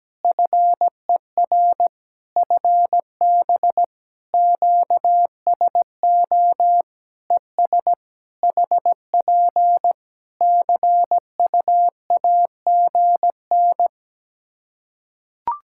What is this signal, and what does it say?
Morse code, heard as a keyed tone, about 17 wpm, 700 Hz. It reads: FER FB QSO ES HP CU AGN
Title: Practice QSO - Exchange 4 - Ending - 17wpm